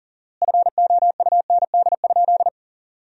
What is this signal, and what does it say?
Morse code, heard as a keyed tone, about 40 wpm, 700 Hz. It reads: FOUND?
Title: 4x Word-Spacing - Sentences from Top 300 Words - 40wpm